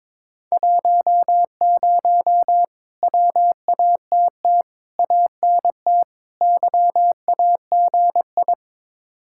Morse code, heard as a keyed tone, about 22 wpm, 700 Hz. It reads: 10 WATT ANT YAGI
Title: Speed-Racing Common QSO Phrases - 22wpm